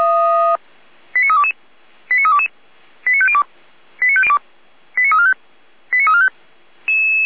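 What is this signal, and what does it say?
Five-tone selective call (ZVEI), heard as quick runs of five tones, tones of about 70 ms. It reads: 89322 89322 89502 89345 89345
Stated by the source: ZVEI five-tone selective call 89340, received on an SDR.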